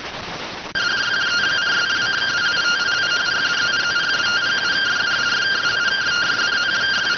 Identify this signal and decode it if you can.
Two-tone teleprinter signal, about 170 Hz shift.